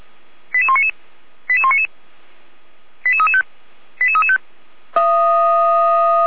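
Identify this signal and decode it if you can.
Five-tone selective call (ZVEI), heard as quick runs of five tones, tones of about 70 ms. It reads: 88199 88199 88335 88335
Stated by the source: ZVEI five-tone selective call received on an SDR, from feuerwehr-orlamuende (fire brigade Orlamünde).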